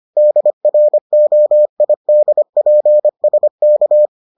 Morse code, heard as a keyed tone, about 25 wpm, 600 Hz. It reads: DROIDPSK